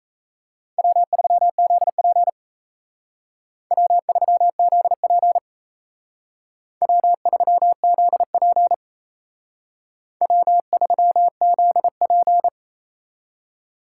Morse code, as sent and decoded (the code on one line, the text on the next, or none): .-- ...-- --.. .--. / .-- ...-- --.. .--. / .-- ...-- --.. .--. / .-- ...-- --.. .--.
W3ZP W3ZP W3ZP W3ZP